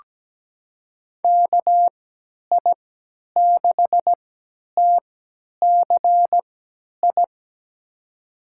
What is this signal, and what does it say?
Morse code, heard as a keyed tone, about 17 wpm, spaced as for 10 wpm, 700 Hz. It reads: KI6TCI
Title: USA Call Signs - 6 Characters - 3x Character Spacing - 17wpm